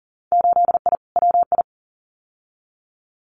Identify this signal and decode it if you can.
Morse code, heard as a keyed tone, about 40 wpm, 700 Hz. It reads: 8I WI
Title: Winter Field Day Exchanges 40wpm